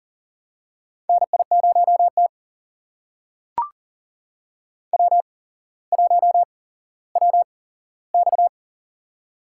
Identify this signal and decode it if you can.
Morse code, heard as a keyed tone, about 40 wpm, spaced as for 17 wpm, 700 Hz. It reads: NI0T W1WX
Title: USA Call Signs - 4 Characters - 8x Character Spacing - 40wpm